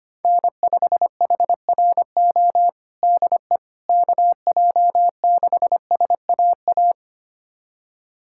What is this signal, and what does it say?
Morse code, heard as a keyed tone, about 25 wpm, 700 Hz. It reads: N5HRO DE KJ6SAA
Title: USA Callsign DE Callsign 25wpm